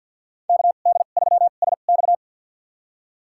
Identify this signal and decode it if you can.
Morse code, heard as a keyed tone, about 52 wpm, spaced as for 36 wpm, 700 Hz. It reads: KD3SX
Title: USA Call Signs - 5 Characters - 2x Character Spacing - 50wpm